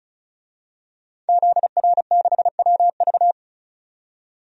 Morse code, heard as a keyed tone, about 36 wpm, 700 Hz. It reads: ZR6WV
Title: DXCC-100 Call Signs - 5 Characters - 35wpm